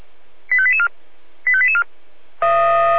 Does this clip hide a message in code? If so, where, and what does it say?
Yes: five-tone selective call (ZVEI) from 0.5 to 1.8 s, heard as quick runs of five tones, tones of about 70 ms. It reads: 85884 85884